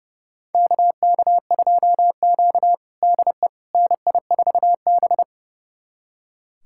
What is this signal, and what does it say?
Morse code, heard as a keyed tone, about 30 wpm, 700 Hz. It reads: KK2Q DE NI4B
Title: USA Callsign DE Callsign - 4 Characters - 30wpm